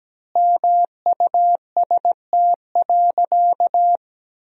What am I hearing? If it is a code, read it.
Morse code, heard as a keyed tone, about 17 wpm, 700 Hz. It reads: MUST.
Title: 2x Word-Spacing - Sentences from Top 200 Words - 17wpm